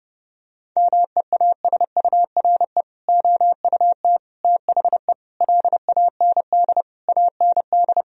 Morse code, heard as a keyed tone, about 30 wpm, 700 Hz. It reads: MEASURE OUT THE LAND AND
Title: Sentences from Top 1000 Words - Review Part 2 - 30wpm